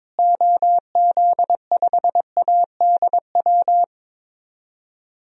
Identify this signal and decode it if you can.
Morse code, heard as a keyed tone, about 22 wpm, 700 Hz. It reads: OZ5ADW
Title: DXCC-100 Call Signs - 6 Characters - Repeated 3x - 22wpm